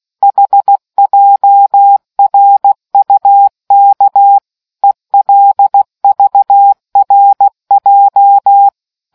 Morse code, heard as a keyed tone, about 16 wpm, 800 Hz. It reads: HJRUK ELVRJ